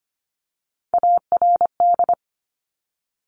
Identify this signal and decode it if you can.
Morse code, heard as a keyed tone, about 25 wpm, 700 Hz. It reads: ARD
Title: Top Three Letter Combinations in Words 25wpm